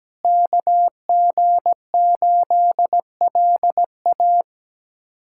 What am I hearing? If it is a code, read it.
Morse code, heard as a keyed tone, about 17 wpm, 700 Hz. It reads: KG8LA
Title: USA Callsign DE Callsign - 5 Characters - 17wpm